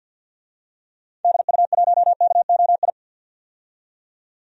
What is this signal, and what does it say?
Morse code, heard as a keyed tone, about 46 wpm, 700 Hz. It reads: DU1KOI
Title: DXCC-100 Call Signs - 6 Characters - 50wpm